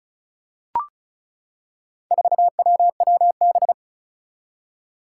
Morse code, heard as a keyed tone, about 36 wpm, 700 Hz. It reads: 4WWB
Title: Four Challenging Letter-Number Combinations 35wpm